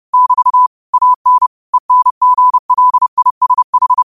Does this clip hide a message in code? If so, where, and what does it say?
Yes: Morse code from 0.1 to 4.0 s, heard as a keyed tone, about 30 wpm, 1000 Hz. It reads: <BT> AN ENGLISH